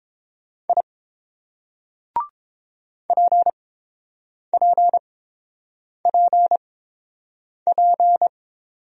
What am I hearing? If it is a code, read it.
Morse code, heard as a keyed tone, about 30 wpm, spaced as for 7 wpm, 700 Hz. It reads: I PPPP